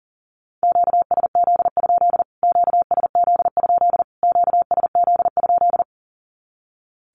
Morse code, heard as a keyed tone, about 40 wpm, 700 Hz. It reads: QSZ? QSZ? QSZ?
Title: Q-Codes Repeated 3x 40wpm